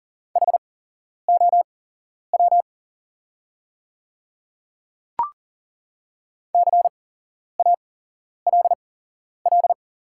Morse code, heard as a keyed tone, about 40 wpm, spaced as for 11 wpm, 700 Hz. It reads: HOW CALL
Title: 8x Character Spacing - Top 200 Words - 40wpm